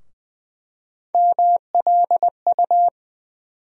Morse code, heard as a keyed tone, about 21 wpm, 700 Hz. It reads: MLU
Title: USA Call Sign Postfixes - 3 Letters 20wpm